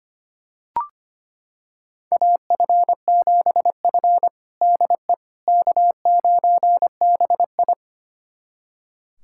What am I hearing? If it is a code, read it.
Morse code, heard as a keyed tone, about 25 wpm, 700 Hz. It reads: AF7F DE K9BI
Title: USA Callsign DE Callsign - 4 Characters - 25wpm